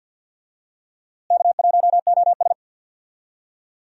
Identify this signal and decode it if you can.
Morse code, heard as a keyed tone, about 51 wpm, 700 Hz. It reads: K1OS
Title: USA Call Signs - 4 Characters - Repeated 3x - 50wpm